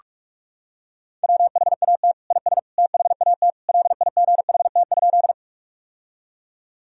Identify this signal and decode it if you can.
Morse code, heard as a keyed tone, about 46 wpm, 700 Hz. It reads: WHAT IS THAT LIGHT?
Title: Sentences from Top 1000 Words - Review Part 1 - 45wpm